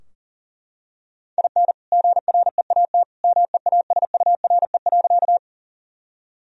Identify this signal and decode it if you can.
Morse code, heard as a keyed tone, about 37 wpm, 700 Hz. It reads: IN GREAT MEASURE.
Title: Sentences from Top 1000 Words - Review Part 2 - 40wpm